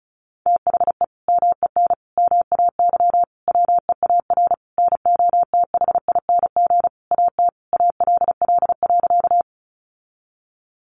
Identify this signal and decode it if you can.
Morse code, heard as a keyed tone, about 34 wpm, 700 Hz. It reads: THE MEN MAY WEAR NOTHING AT ALL.
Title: Sentences from Top 1500 Words 35wpm